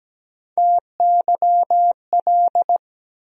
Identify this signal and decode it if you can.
Morse code, heard as a keyed tone, about 17 wpm, 700 Hz. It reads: TYL
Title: USA Call Sign Postfixes - 3 Letters 17wpm